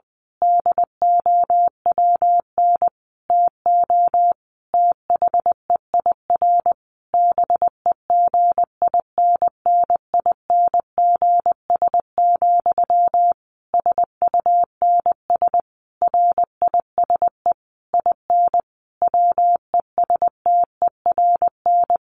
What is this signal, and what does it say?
Morse code, heard as a keyed tone, about 20 wpm, 700 Hz. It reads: DOWN TO THEIR BEGINNINGS, SUNS RISE IN WESTERN
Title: A Sound Of Thunder 20wpm